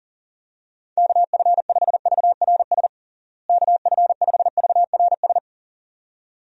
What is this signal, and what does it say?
Morse code, heard as a keyed tone, about 40 wpm, 700 Hz. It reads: KF5VRS KF5VRS
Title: USA Call Signs - 6 Characters - Repeated 2x - 40wpm